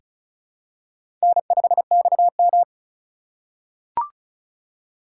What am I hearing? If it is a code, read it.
Morse code, heard as a keyed tone, about 36 wpm, 700 Hz. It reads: N5XM
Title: USA Call Signs - 4 Characters - 4x Character Spacing - 35wpm